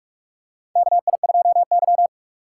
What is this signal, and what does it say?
Morse code, heard as a keyed tone, about 45 wpm, 690 Hz. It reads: KI2Y
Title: USA Call Signs - 4 Characters - 3x Character Spacing - 45wpm